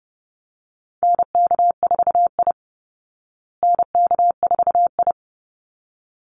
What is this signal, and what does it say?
Morse code, heard as a keyed tone, about 30 wpm, 700 Hz. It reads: NK4I NK4I
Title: Call Signs Repeated 3x - 30wpm